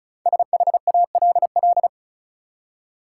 Morse code, heard as a keyed tone, about 34 wpm, 700 Hz. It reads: SHALL
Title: Five Letter Words - 35wpm